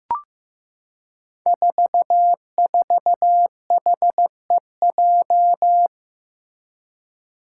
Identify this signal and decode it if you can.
Morse code, heard as a keyed tone, about 15 wpm, 700 Hz. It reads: 44HEJ
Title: Five Challenging Letter-Number Combinations 15wpm